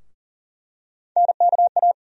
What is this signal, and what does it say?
Morse code, heard as a keyed tone, about 40 wpm, 700 Hz. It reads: NKA
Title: USA Call Sign Postfixes - 3 Letters 40wpm